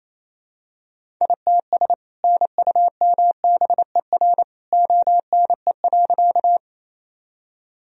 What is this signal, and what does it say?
Morse code, heard as a keyed tone, about 28 wpm, 700 Hz. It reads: ITS NUMBER ONE.
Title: Sentences from Top 1000 Words - Review Part 1 - 28wpm